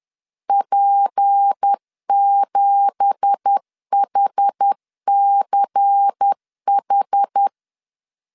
Morse code, heard as a keyed tone, about 10 wpm, 800 Hz. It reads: P7HCH
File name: LEZIONI / 10 SIGNS / 10C DOT 10wpm 2940